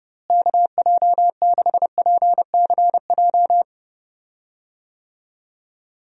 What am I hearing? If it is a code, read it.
Morse code, heard as a keyed tone, about 30 wpm, 700 Hz. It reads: KJ6PCJ
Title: USA Call Signs - 6 Characters - Repeated 2x - 30wpm